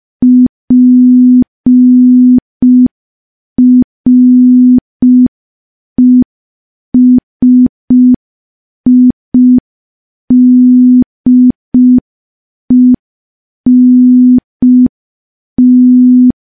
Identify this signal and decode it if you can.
Morse code, heard as a keyed tone, about 5 wpm, 250 Hz. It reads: PRESIDENT
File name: morse / morse-training-05WPM